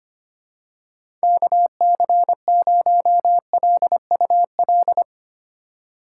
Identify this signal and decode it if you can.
Morse code, heard as a keyed tone, about 25 wpm, 700 Hz. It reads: KC0LUL
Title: USA Call Signs - 6 Characters - 6x Character Spacing - 25wpm